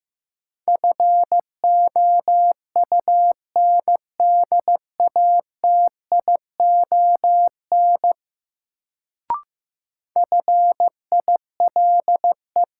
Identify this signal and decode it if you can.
Morse code, heard as a keyed tone, about 15 wpm, 700 Hz. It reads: FOUNDATION FILE
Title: Sentences from Top 1200 Words - New Word Drill Alt - 15wpm